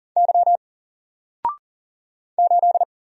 Morse code, heard as a keyed tone, about 38 wpm, 700 Hz. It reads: Y 8
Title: Single Letter-Number 40wpm